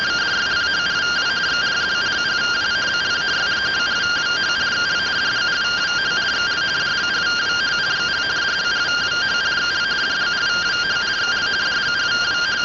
Two-tone teleprinter signal, about 170 Hz shift.